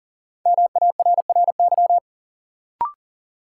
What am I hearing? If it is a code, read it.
Morse code, heard as a keyed tone, about 40 wpm, 700 Hz. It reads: MARRY
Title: Sentences from Top 1200 Words - New Word Drill Alt - 40wpm